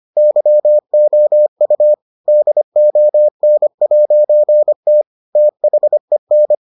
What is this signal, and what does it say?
Morse code, heard as a keyed tone, about 25 wpm, 600 Hz. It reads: YOU DON'T THEN